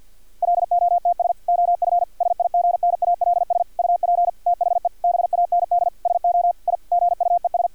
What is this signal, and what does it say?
Morse code, heard as a keyed tone, about 50 wpm, 700 Hz, with radio noise. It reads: LOTS OF SIGNALS UP THE BAND SO I GUES
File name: archived / CRSnet